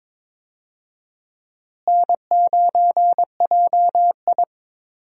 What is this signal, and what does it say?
Morse code, heard as a keyed tone, about 22 wpm, 700 Hz. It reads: N9JI
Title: USA Call Signs - 4 Characters - Repeated 3x - 22wpm